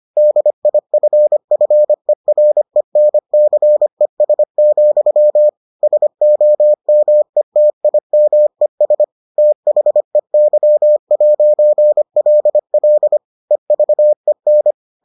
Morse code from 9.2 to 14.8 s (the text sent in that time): THEY'LL EVEN